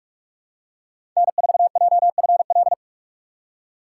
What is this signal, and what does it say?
Morse code, heard as a keyed tone, about 44 wpm, 700 Hz. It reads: N4JFL
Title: USA Call Signs - 5 Characters - 3x Character Spacing - 45wpm